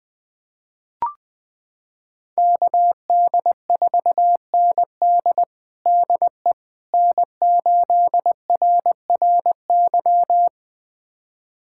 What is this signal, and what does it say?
Morse code, heard as a keyed tone, about 20 wpm, 700 Hz. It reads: KD4ND DE N8RRY
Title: USA Callsign DE Callsign - 5 Characters - 20wpm